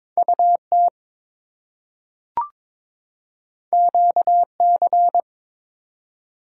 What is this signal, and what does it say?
Morse code, heard as a keyed tone, about 22 wpm, 700 Hz. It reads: UT QC
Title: US States and Canadian Provinces 22wpm